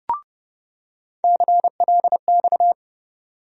Morse code, heard as a keyed tone, about 30 wpm, 700 Hz. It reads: CLX
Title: USA Call Sign Postfixes - 3 Letters 30wpm